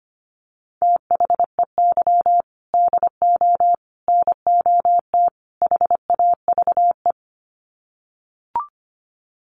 Morse code, heard as a keyed tone, about 25 wpm, 700 Hz. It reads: THEY DO NOT HAVE
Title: Sets of 4 Words 25wpm